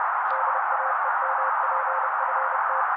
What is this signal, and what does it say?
Morse code, heard as a keyed tone, about 30 wpm, 530 Hz. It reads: DAEMPFT
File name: daily / 29wpm noise